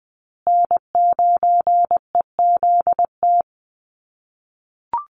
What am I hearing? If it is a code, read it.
Morse code, heard as a keyed tone, about 20 wpm, 700 Hz. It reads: N9EZT
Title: USA Call Signs Repeated 3x 20wpm